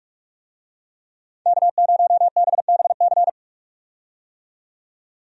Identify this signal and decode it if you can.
Morse code, heard as a keyed tone, about 43 wpm, 700 Hz. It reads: K0BBC
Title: USA Call Signs - 5 Characters - 45wpm